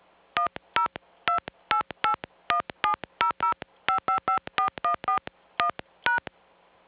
DTMF, heard as telephone touch tones, tones of about 110 ms. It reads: 1028817005554141#